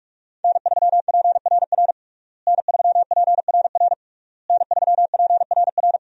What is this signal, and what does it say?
Morse code, heard as a keyed tone, about 41 wpm, 700 Hz. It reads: N3PRR N3PRR N3PRR
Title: USA Call Signs - 5 Characters - Repeated 3x - 45wpm